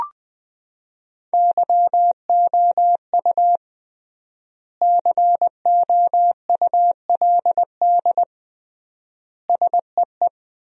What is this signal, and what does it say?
Morse code, heard as a keyed tone, about 20 wpm, 700 Hz. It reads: YOU COULD SEE
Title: 3x Word-Spacing - Sentences from Top 500 Words - 20wpm